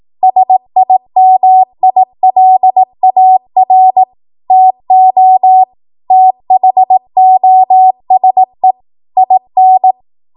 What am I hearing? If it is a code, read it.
Morse code, heard as a keyed tone, about 18 wpm, 750 Hz. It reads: SIMILAR TO THOSE IN
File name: Morse / 18WPM